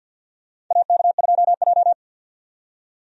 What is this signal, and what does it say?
Morse code, heard as a keyed tone, about 52 wpm, 700 Hz. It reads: AK2J
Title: USA Call Signs - 4 Characters - 50wpm